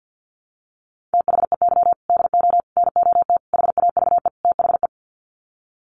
Morse code, heard as a keyed tone, about 50 wpm, 700 Hz. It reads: THEY DO NOT HAVE THE
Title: Sets of 5 Words 50wpm